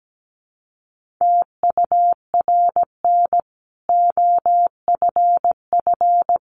Morse code, heard as a keyed tone, about 17 wpm, 700 Hz. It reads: TURN OFF
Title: Sentences from Top 1400 Words 17wpm